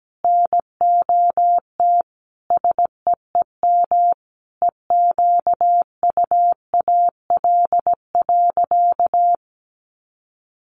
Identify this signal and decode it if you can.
Morse code, heard as a keyed tone, about 17 wpm, 700 Hz. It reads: NOT SEEM EQUAL.
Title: Sentences from Top 1400 Words 17wpm